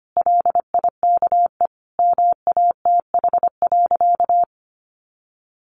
Morse code, heard as a keyed tone, about 25 wpm, 700 Hz. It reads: LIKE MATH.
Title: Easy Phrases 25wpm